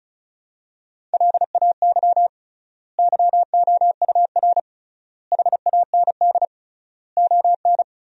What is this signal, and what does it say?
Morse code, heard as a keyed tone, about 34 wpm, 700 Hz. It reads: LAY YOUR HAND ON
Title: 3x Word-Spacing - Sentences from Top 400 Words - 35wpm